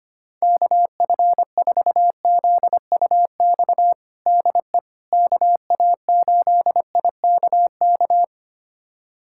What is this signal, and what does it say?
Morse code, heard as a keyed tone, about 25 wpm, 700 Hz. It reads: KF4ZUX DE KA8IKK
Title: USA Callsign DE Callsign - 6 Characters - 25wpm